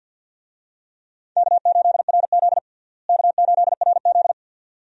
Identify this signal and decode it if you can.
Morse code, heard as a keyed tone, about 50 wpm, 700 Hz. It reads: K8RZ K8RZ
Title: USA Call Signs - 4 Characters - Repeated 2x - 50wpm